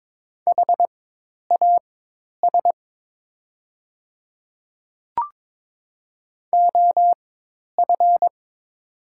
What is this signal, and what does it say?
Morse code, heard as a keyed tone, about 22 wpm, spaced as for 10 wpm, 700 Hz. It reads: HAS OF
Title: 4x Character Spacing - Top 300 Words - 22wpm